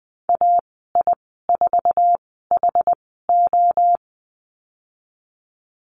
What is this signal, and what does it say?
Morse code, heard as a keyed tone, about 20 wpm, spaced as for 14 wpm, 700 Hz. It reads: AI4HO
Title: 069 Lesson 12g - Letter d - Call Signs 20wpm